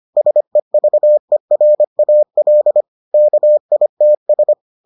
Morse code, heard as a keyed tone, about 25 wpm, 600 Hz. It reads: SEVERAL KITS